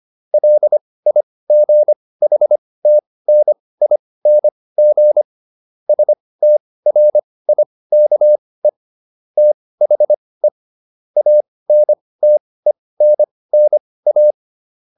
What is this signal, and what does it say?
Morse code, heard as a keyed tone, about 25 wpm, spaced as for 18 wpm, 600 Hz. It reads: LIGHTNING STRIKE THE ANTENNA